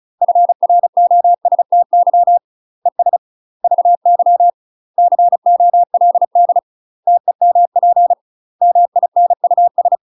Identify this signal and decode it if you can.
Morse code, heard as a keyed tone, about 36 wpm, 700 Hz. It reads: FROSTY ES VY COLD TEMP MINUS
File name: UK MASTER QSOs TRAINING / 35WPM / UK MASTER QSOs 0035